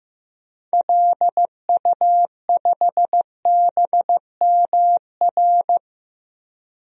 Morse code, heard as a keyed tone, about 15 wpm, 700 Hz. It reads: LU5BMR
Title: DXCC-100 Call Signs - 6 Characters - 15wpm